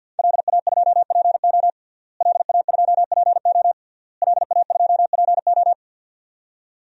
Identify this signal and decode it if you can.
Morse code, heard as a keyed tone, about 46 wpm, 700 Hz. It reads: LA2PO LA2PO LA2PO
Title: DXCC-100 Call Signs - 5 Characters - Repeated 3x - 50wpm